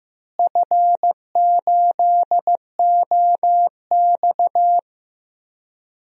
Morse code, heard as a keyed tone, about 15 wpm, 700 Hz. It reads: F8OX
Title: DXCC-100 Call Signs - 4 Characters - Repeated 2x - 15wpm